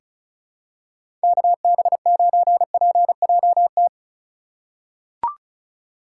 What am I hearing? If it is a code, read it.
Morse code, heard as a keyed tone, about 34 wpm, 700 Hz. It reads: KB9PJT
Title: USA Call Signs - 6 Characters - 3x Character Spacing - 35wpm